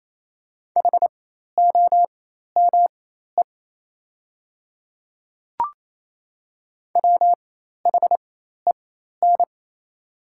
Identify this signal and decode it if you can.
Morse code, heard as a keyed tone, about 28 wpm, spaced as for 13 wpm, 700 Hz. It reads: HOME WHEN
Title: 4x Character Spacing - Top 200 Words - 28wpm